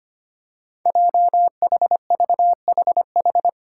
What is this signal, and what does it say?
Morse code, heard as a keyed tone, about 25 wpm, 700 Hz. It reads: JHVHH
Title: Five Challenging Letter-Number Combinations 25wpm